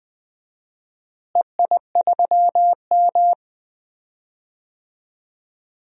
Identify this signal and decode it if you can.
Morse code, heard as a keyed tone, about 20 wpm, 700 Hz. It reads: EI3M
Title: DXCC-100 Call Signs - 4 Characters - Repeated 3x - 20wpm